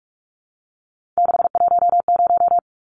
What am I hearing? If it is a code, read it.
Morse code, heard as a keyed tone, about 44 wpm, 700 Hz. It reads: 610
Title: Numbers - 3-Digit 45wpm